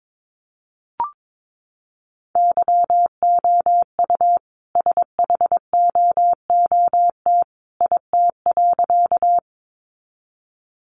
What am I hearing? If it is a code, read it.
Morse code, heard as a keyed tone, about 22 wpm, 700 Hz. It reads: YOU SHOOT IT.
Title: A Sound Of Thunder 22wpm